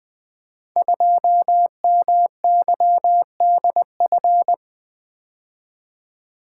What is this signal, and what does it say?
Morse code, heard as a keyed tone, about 20 wpm, 700 Hz. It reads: 2MYDF
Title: Code Groups - 5 Letter:Numbers - 20wpm